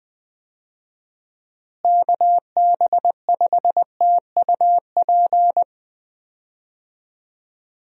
Morse code, heard as a keyed tone, about 20 wpm, 700 Hz. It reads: KB5TUP
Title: USA Call Signs - 6 Characters - Repeated 2x - 20wpm